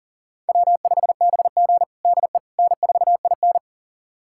Morse code, heard as a keyed tone, about 40 wpm, 700 Hz. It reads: W5BG DE N4IN
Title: USA Callsign DE Callsign - 4 Characters - 40wpm